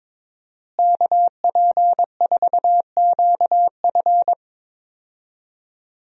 Morse code, heard as a keyed tone, about 22 wpm, 700 Hz. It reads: KP4QF
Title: DXCC-100 Call Signs - 5 Characters - Repeated 2x - 22wpm